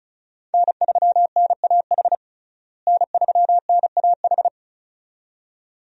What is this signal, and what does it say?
Morse code, heard as a keyed tone, about 36 wpm, 700 Hz. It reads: N3NAH N3NAH
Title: USA Call Signs - 5 Characters - Repeated 2x - 35wpm